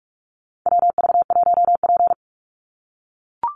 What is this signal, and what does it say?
Morse code, heard as a keyed tone, about 44 wpm, 700 Hz. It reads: WV1P
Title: USA Call Signs 45wpm